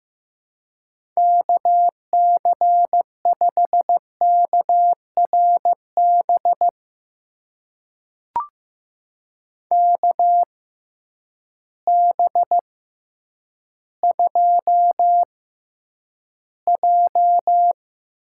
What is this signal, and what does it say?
Morse code, heard as a keyed tone, about 15 wpm, spaced as for 9 wpm, 700 Hz. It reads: KC5KRB KB2J
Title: USA Call Signs - 6 Characters - 6x Character Spacing - 15wpm